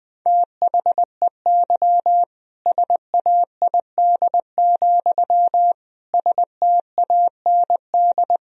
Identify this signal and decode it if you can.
Morse code, heard as a keyed tone, about 19 wpm, 700 Hz. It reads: THEY SAID, STAND
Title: Sentences from Top 1000 Words - Review Part 1 - 20wpm